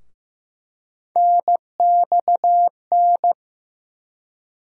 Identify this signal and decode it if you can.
Morse code, heard as a keyed tone, about 16 wpm, 700 Hz. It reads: NXN
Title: USA Call Sign Postfixes - 3 Letters 15wpm